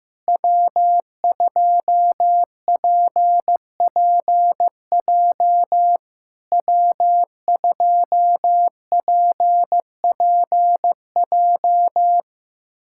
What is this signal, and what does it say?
Morse code, heard as a keyed tone, about 15 wpm, 700 Hz. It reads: W2PPJ W2PPJ
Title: SOTA QSO Practice 15wpm